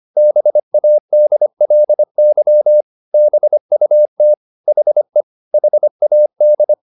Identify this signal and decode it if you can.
Morse code, heard as a keyed tone, about 25 wpm, 600 Hz. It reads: BADLY BUT HE HAD